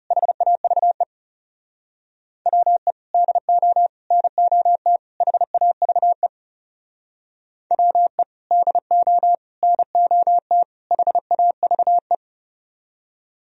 Morse code, heard as a keyed tone, about 36 wpm, 700 Hz. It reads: HAVE WE DO NOT HAVE WE DO NOT HAVE
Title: Speed-Racing - Sets of 4 Words 30wpm